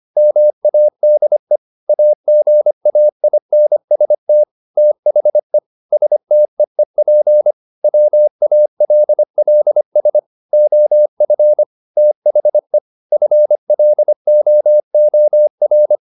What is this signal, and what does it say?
Morse code, heard as a keyed tone, about 25 wpm, 600 Hz. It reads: MADE AGAINST THE STEEP WALLS OF THE FLOOR
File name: the old man and the sea part 1 25wpm